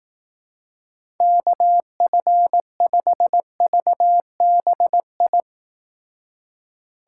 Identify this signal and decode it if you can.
Morse code, heard as a keyed tone, about 18 wpm, 700 Hz. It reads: KF5VBI
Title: Speed Racing - USA 2x3 Call Signs 15wpm